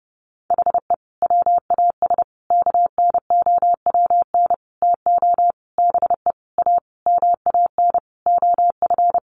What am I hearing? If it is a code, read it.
Morse code, heard as a keyed tone, about 28 wpm, 700 Hz. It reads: HE WAS KNOWN TO BE A MAN OF
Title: Sentences from Top 1200 Words 30wpm